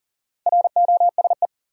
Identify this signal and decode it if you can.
Morse code, heard as a keyed tone, about 40 wpm, 700 Hz. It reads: ROSE
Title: USA Common Names - Max 5 Letters - 40wpm